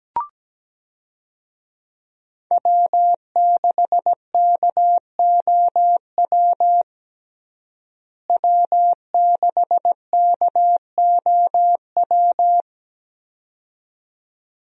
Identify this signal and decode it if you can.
Morse code, heard as a keyed tone, about 17 wpm, 700 Hz. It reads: W6KOW W6KOW
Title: USA Call Signs - 5 Characters - Repeated 2x - 17wpm